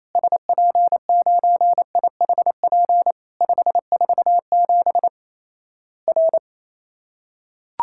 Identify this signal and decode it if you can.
Morse code, heard as a keyed tone, about 28 wpm, 700 Hz. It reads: SP9IHP 547 R
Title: Worked All Europe DX Contest - QTC Exchange - 28wpm